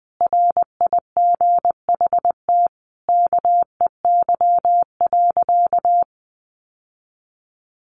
Morse code, heard as a keyed tone, about 20 wpm, 700 Hz. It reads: RIGHT KEY.
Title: Sentences from Top 600 Words 20wpm